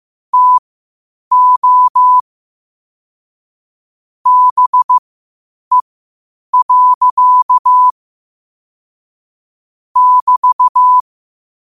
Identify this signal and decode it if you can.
Morse code, heard as a keyed tone, about 15 wpm, spaced as for 8 wpm, 1000 Hz. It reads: TO BE. <BT>
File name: qotdfiles / 8wpm